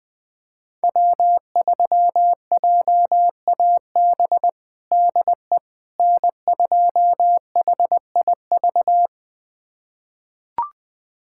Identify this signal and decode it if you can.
Morse code, heard as a keyed tone, about 20 wpm, 700 Hz. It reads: W3JAB DE N2HIV
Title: USA Callsign DE Callsign - 5 Characters - 20wpm